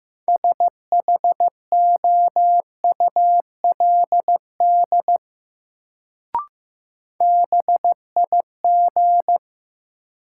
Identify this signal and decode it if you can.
Morse code, heard as a keyed tone, about 15 wpm, 700 Hz. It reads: SHOULD BIG
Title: Sets of 2 Words - New Word Drill - 15wpm